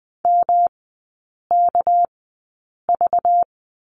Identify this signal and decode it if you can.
Morse code, heard as a keyed tone, about 19 wpm, spaced as for 8 wpm, 700 Hz. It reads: MKV
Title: 205 Lesson 33e - Letter j - Three Characters 20wpm